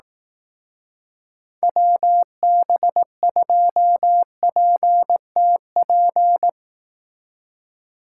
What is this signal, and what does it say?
Morse code, heard as a keyed tone, about 18 wpm, 700 Hz. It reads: WB2PTP
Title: Speed Racing - USA 2x3 Call Signs 15wpm